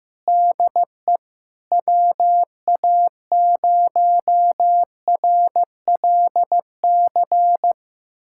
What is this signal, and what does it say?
Morse code, heard as a keyed tone, about 15 wpm, 700 Hz. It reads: DE WA0RLC
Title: USA Callsign DE Callsign 15wpm